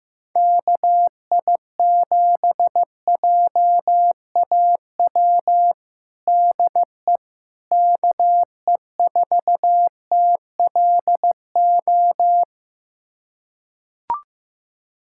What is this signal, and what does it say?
Morse code, heard as a keyed tone, about 15 wpm, 700 Hz. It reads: KI7JAW DE KE4TLO
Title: USA Callsign DE Callsign 15wpm